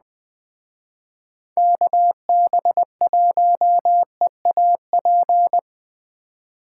Morse code, heard as a keyed tone, about 21 wpm, 700 Hz. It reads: KB1EAP